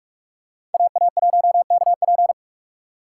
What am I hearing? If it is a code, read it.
Morse code, heard as a keyed tone, about 41 wpm, 700 Hz. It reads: AA1KP
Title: USA Call Signs - 5 Characters - 4x Character Spacing - 45wpm